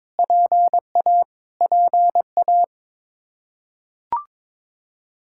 Morse code, heard as a keyed tone, about 22 wpm, 700 Hz. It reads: PA PA
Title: Practice QSO - Exchange 1 - Beginning - 22wpm